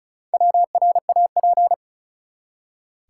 Morse code, heard as a keyed tone, about 36 wpm, 700 Hz. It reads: WRAP
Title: Four Letter Words - 35wpm